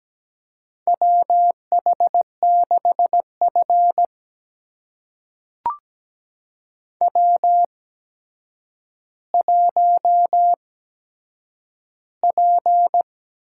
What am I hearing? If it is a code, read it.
Morse code, heard as a keyed tone, about 17 wpm, spaced as for 8 wpm, 700 Hz. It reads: WH6F W1P
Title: USA Call Signs - 4 Characters - 8x Character Spacing - 17wpm